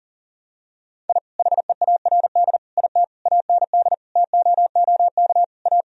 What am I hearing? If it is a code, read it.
Morse code, heard as a keyed tone, about 40 wpm, 700 Hz. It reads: I HEARD IT AND TOOK A
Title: Sentences from Top 1000 Words - Review Part 1 - 40wpm